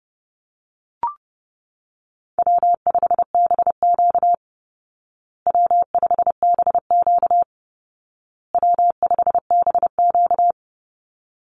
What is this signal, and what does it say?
Morse code, heard as a keyed tone, about 30 wpm, 700 Hz. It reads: W5BQ W5BQ W5BQ
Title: Easy Call Signs Repeated 3x - 30wpm